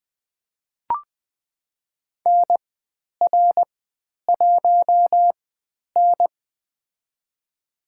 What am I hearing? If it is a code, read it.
Morse code, heard as a keyed tone, about 20 wpm, spaced as for 10 wpm, 700 Hz. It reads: NR1N
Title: 102 Lesson 17g - Letter u - Call Signs 20wpm